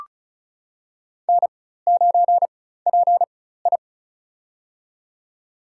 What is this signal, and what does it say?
Morse code, heard as a keyed tone, about 36 wpm, spaced as for 16 wpm, 700 Hz. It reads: N9PI
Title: USA Call Signs - 4x Character Spacing - 35wpm